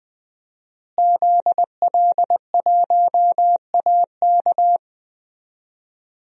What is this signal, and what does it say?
Morse code, heard as a keyed tone, about 20 wpm, 700 Hz. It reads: ZL1AK
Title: DXCC-100 Call Signs - 5 Characters - Repeated 3x - 20wpm